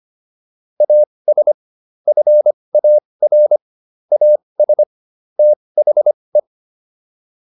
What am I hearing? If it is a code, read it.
Morse code, heard as a keyed tone, about 25 wpm, spaced as for 20 wpm, 600 Hz. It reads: AS FAR AS THE